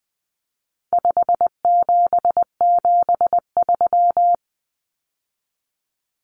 Morse code, heard as a keyed tone, about 20 wpm, 700 Hz. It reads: 5773
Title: Common Cut Numbers - 4-Digit 20wpm